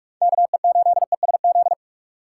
Morse code, heard as a keyed tone, about 44 wpm, 690 Hz. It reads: KE8ESZ
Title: USA Call Signs - 6 Characters - 3x Character Spacing - 45wpm